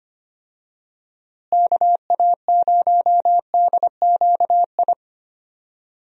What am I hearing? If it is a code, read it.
Morse code, heard as a keyed tone, about 25 wpm, 700 Hz. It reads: KA0DQI
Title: USA Call Signs - 6 Characters - Repeated 2x - 25wpm